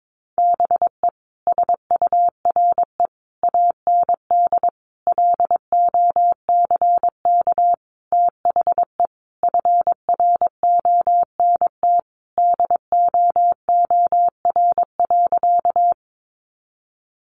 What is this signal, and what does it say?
Morse code, heard as a keyed tone, about 22 wpm, 700 Hz. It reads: BE SURE AND LOCK THE FRONT DOOR.
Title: Sentences from Top 1300 Words 22wpm